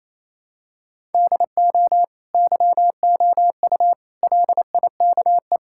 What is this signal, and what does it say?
Morse code, heard as a keyed tone, about 28 wpm, 700 Hz. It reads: DO YOU LIKE
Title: Sentences from Top 1000 Words - Review Part 5 - 28wpm